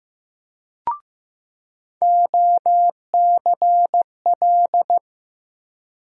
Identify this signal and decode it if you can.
Morse code, heard as a keyed tone, about 15 wpm, 700 Hz. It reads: OCL
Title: USA Call Sign Postfixes - 3 Letters 15wpm